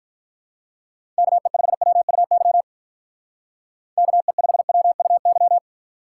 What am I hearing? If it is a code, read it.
Morse code, heard as a keyed tone, about 45 wpm, 700 Hz. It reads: KE5WUY KE5WUY